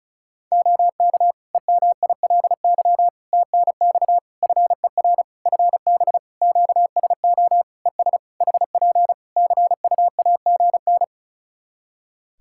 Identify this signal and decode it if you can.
Morse code, heard as a keyed tone, about 34 wpm, 700 Hz. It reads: OK EMILY TNX FER FB QSO ES HP CUAGN
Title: Practice QSO - Exchange 4 - Ending - 35wpm